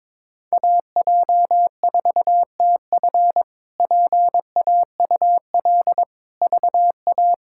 Morse code, heard as a keyed tone, about 22 wpm, 700 Hz. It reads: AJ4TF PAUL VA
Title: K1USN SST QSO Practice - 22wpm